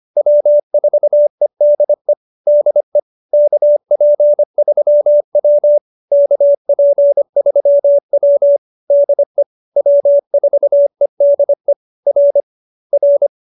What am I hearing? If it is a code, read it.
Morse code, heard as a keyed tone, about 25 wpm, 570 Hz. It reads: W4EDE DE KP3W KP3W DE W4EDE R R